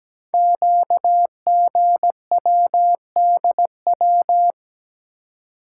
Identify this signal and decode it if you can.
Morse code, heard as a keyed tone, about 17 wpm, 700 Hz. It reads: QGWDW
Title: Code Groups - 5 Letters - 17wpm